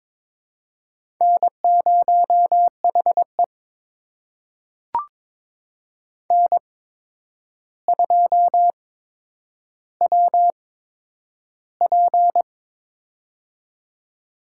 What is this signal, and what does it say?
Morse code, heard as a keyed tone, about 23 wpm, spaced as for 9 wpm, 700 Hz. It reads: N0HE N2WP
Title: USA Call Signs - 4 Characters - 8x Character Spacing - 22wpm